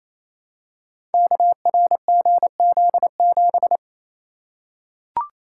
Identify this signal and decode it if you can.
Morse code, heard as a keyed tone, about 29 wpm, 700 Hz. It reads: KRGZ7
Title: Code Groups - 5 Letter:Numbers - 28wpm